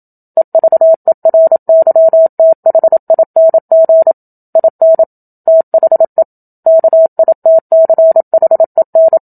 Morse code, heard as a keyed tone, about 27 wpm, 650 Hz. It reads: EVERYTHING IN THE KITCHEN
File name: Book Two / 01-Chapter 1 - 27 WPM